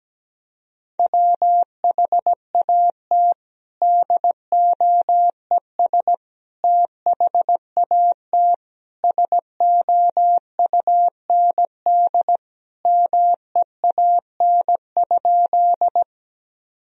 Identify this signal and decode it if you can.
Morse code, heard as a keyed tone, about 17 wpm, 700 Hz. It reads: WHAT DOES THAT SOUND MEAN?
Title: Sentences from Top 1000 Words - Review Part 1 - 17wpm